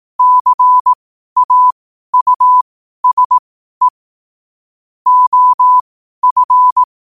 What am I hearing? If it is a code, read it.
Morse code, heard as a keyed tone, about 18 wpm, spaced as for 12 wpm, 1000 Hz. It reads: CAUSE OF
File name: qotdfiles / 12wpm